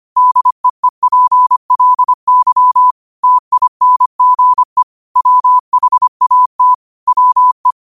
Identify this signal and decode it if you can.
Morse code, heard as a keyed tone, about 25 wpm, 1000 Hz. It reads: DEEPLY TINGE WHAT WE